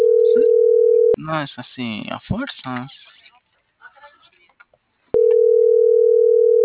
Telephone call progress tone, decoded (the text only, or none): ringback tone